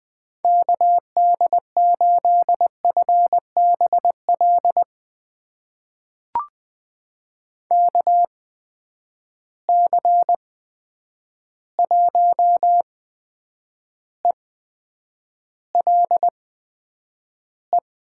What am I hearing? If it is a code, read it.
Morse code, heard as a keyed tone, about 21 wpm, spaced as for 9 wpm, 700 Hz. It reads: KD8FBL KC1ELE
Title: USA Call Signs - 6 Characters - 8x Character Spacing - 20wpm